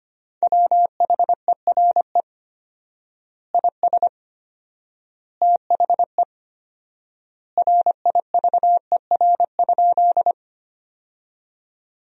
Morse code, heard as a keyed tone, about 25 wpm, 700 Hz. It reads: WHERE IS THE RIVER?
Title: 4x Word-Spacing - Sentences from Top 300 Words - 25wpm